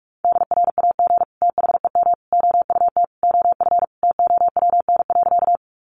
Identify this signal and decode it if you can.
Morse code, heard as a keyed tone, about 44 wpm, 700 Hz. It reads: DRAG THEM OUT OF TOWN.
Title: Sentences from Top 1500 Words 45wpm